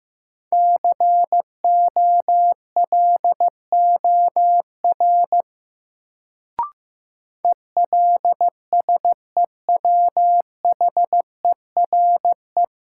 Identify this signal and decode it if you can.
Morse code, heard as a keyed tone, about 15 wpm, 700 Hz. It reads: COLOR ELSEWHERE
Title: Sentences from Top 1600 Words - New Word Drill Alt - 15wpm